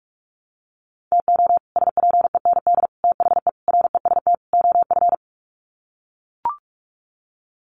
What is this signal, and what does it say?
Morse code, heard as a keyed tone, about 44 wpm, 700 Hz. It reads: TO SPEND THE REST OF
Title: Sets of 5 Words Encore2 45wpm